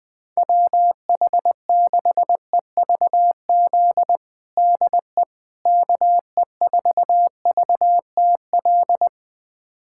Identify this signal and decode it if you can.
Morse code, heard as a keyed tone, about 20 wpm, 700 Hz. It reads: WH6EVZ DE KE4VTL
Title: USA Callsign DE Callsign - 6 Characters - 20wpm